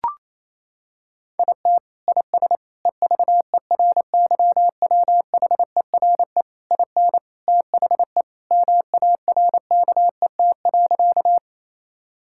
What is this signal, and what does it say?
Morse code, heard as a keyed tone, about 28 wpm, 700 Hz. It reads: IT IS EVERYWHERE IN THE MARKET.
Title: Sentences from Top 1600 Words 28wpm